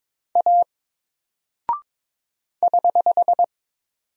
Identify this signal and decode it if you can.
Morse code, heard as a keyed tone, about 23 wpm, 700 Hz. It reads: A <HH>
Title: Single Character (Letter:Number:Punctuation:Prosign) 22wpm